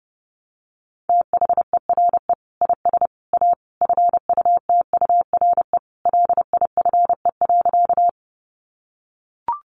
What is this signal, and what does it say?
Morse code, heard as a keyed tone, about 30 wpm, 700 Hz. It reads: THERE IS A FUTURE LIFE.
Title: Sentences from Top 700 Words 30wpm